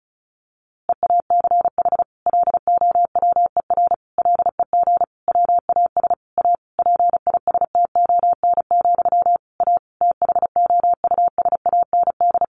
Morse code, heard as a keyed tone, about 34 wpm, 700 Hz. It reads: EACH LOWER LEG WAS A PISTON, A THOUSAND
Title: A Sound Of Thunder 35wpm